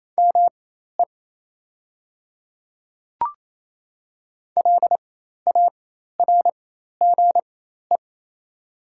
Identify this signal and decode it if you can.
Morse code, heard as a keyed tone, about 28 wpm, spaced as for 13 wpm, 690 Hz. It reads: ME LARGE
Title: 4x Character Spacing - Top 200 Words - 28wpm